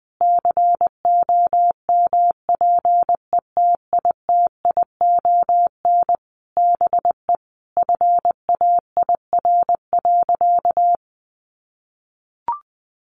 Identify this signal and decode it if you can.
Morse code, heard as a keyed tone, about 19 wpm, 700 Hz. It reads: COMPETITION BE FAIR.
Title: Sentences from Top 1300 Words 20wpm